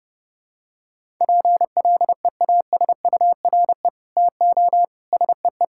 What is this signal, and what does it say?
Morse code, heard as a keyed tone, about 32 wpm, 700 Hz. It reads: PLEASURE TO SEE
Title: Sentences from Top 1600 Words 30wpm